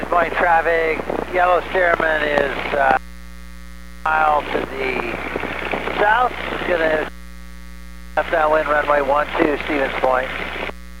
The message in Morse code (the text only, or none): none